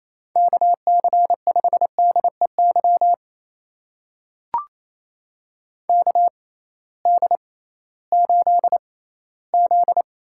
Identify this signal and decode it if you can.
Morse code, heard as a keyed tone, about 28 wpm, spaced as for 16 wpm, 700 Hz. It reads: KC5DEY KD8Z
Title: USA Call Signs - 6 Characters - 6x Character Spacing - 28wpm